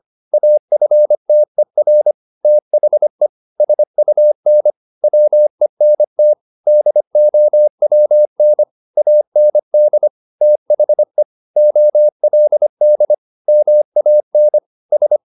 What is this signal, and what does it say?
Morse code, heard as a keyed tone, about 25 wpm, 600 Hz. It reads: AFTER THE SUN WENT DOWN AND THE OLD MAN S
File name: the old man and the sea part 1 25wpm